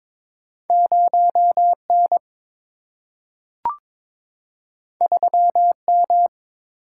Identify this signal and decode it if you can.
Morse code, heard as a keyed tone, about 22 wpm, 700 Hz. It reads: 0 N 3 M
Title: Single-Letter Single-Number 22wpm